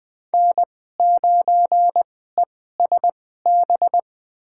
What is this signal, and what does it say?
Morse code, heard as a keyed tone, about 20 wpm, spaced as for 14 wpm, 700 Hz. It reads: N9ESB
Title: USA Call Signs - 5 Characters - 2x Character Spacing - 20wpm